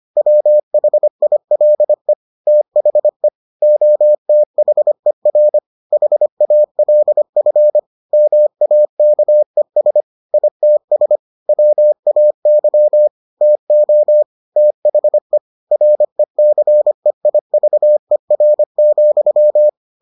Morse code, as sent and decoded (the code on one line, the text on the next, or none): .-- .... .. .-.. / - .... / --- - .... .-. / .... .- .-.. ..-. / -- .- -.- . ... / .. - ... / .-- .- -.-- / - --- / - .... / .-. -.-. .. ...- . .-. --..--
WHILE THE OTHER HALF MAKES ITS WAY TO THE RECEIVER,